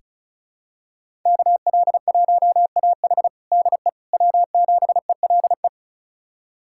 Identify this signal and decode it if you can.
Morse code, heard as a keyed tone, about 34 wpm, 700 Hz. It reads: KL1AH DE W7ELE
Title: USA Callsign DE Callsign 35wpm